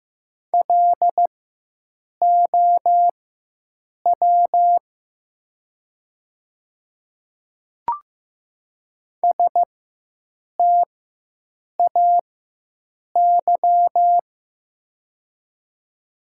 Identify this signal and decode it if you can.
Morse code, heard as a keyed tone, about 15 wpm, spaced as for 7 wpm, 700 Hz. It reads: LOW STAY
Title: 4x Character Spacing - Top 500 Words - 15wpm